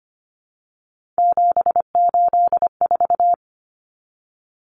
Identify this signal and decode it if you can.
Morse code, heard as a keyed tone, about 25 wpm, 700 Hz. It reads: 784
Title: Numbers - 3-Digit 25wpm